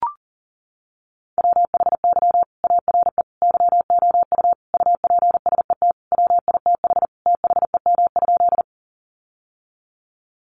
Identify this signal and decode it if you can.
Morse code, heard as a keyed tone, about 40 wpm, 700 Hz. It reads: WHY ARE YOU UPSET WITH THEM?
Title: Sentences from Top 800 Words 40wpm